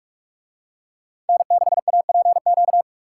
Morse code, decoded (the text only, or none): N6APQ